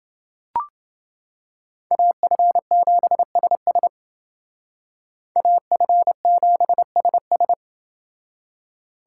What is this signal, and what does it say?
Morse code, heard as a keyed tone, about 29 wpm, 700 Hz. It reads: AF7SS AF7SS